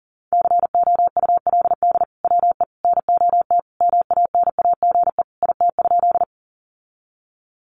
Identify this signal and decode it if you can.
Morse code, heard as a keyed tone, about 37 wpm, 700 Hz. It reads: COULD WE NOT MANAGE IT?
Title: Sentences from Top 1400 Words 40wpm